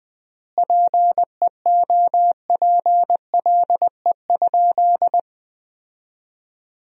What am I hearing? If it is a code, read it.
Morse code, heard as a keyed tone, about 20 wpm, 700 Hz. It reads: PEOPLE?
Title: 4x Word-Spacing - Sentences from Top 200 Words - 20wpm